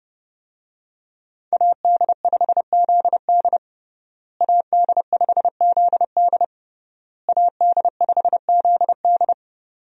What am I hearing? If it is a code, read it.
Morse code, heard as a keyed tone, about 32 wpm, 700 Hz. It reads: AD5ZD AD5ZD AD5ZD
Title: USA Call Signs - 5 Characters - Repeated 3x - 30wpm